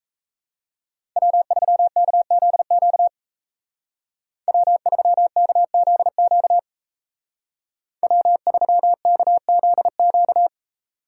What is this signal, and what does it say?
Morse code, heard as a keyed tone, about 36 wpm, 700 Hz. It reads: W3KZQ W3KZQ W3KZQ